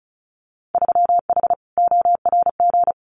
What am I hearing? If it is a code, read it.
Morse code, heard as a keyed tone, about 34 wpm, 700 Hz. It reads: 3H ORG
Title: Winter Field Day Exchanges Encore 35wpm